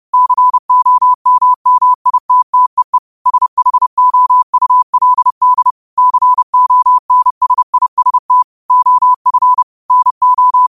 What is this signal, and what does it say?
Morse code, heard as a keyed tone, about 31 wpm, 1000 Hz. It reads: COMMITTEE SHOULD CONSIST OF NO
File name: qotdfiles / 30wpm